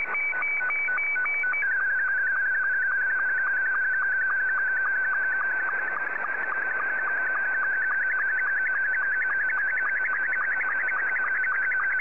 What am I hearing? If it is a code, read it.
SSTV (slow-scan television), Scottie 2.
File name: sstv / S2-VK2GHO